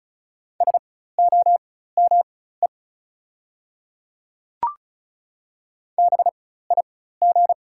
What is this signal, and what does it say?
Morse code, heard as a keyed tone, about 36 wpm, spaced as for 16 wpm, 700 Hz. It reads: SOME BIG
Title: 4x Character Spacing - Top 300 Words - 35wpm